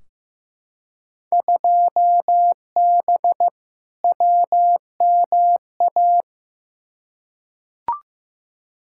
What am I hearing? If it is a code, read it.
Morse code, heard as a keyed tone, about 15 wpm, 700 Hz. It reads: 2B WMA
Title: Field Day Exchanges Encore - 15wpm